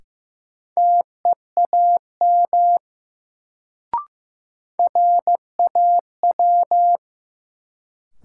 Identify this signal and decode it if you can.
Morse code, heard as a keyed tone, about 15 wpm, 700 Hz. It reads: TEAM RAW
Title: Sentences from Top 1500 Words - New Word Drill Alt - 15wpm